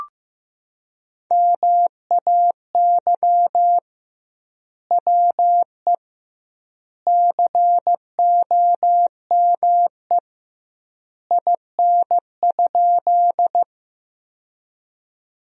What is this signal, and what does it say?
Morse code, heard as a keyed tone, about 15 wpm, 700 Hz. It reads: MAY WE COME IN?
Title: 2x Word-Spacing - Sentences from Top 100 Words - 15wpm